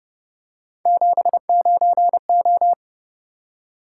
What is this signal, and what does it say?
Morse code, heard as a keyed tone, about 30 wpm, 700 Hz. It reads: 79O